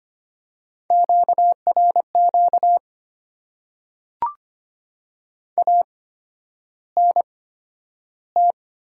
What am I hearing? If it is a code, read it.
Morse code, heard as a keyed tone, about 25 wpm, spaced as for 11 wpm, 700 Hz. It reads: QRQ ANT
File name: Top 90 QSO Elements - 8x Character Spacing - 25wpm